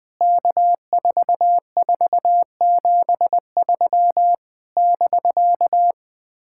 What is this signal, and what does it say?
Morse code, heard as a keyed tone, about 20 wpm, 700 Hz. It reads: K4473 <BK>
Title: POTA QSO Practice - 20wpm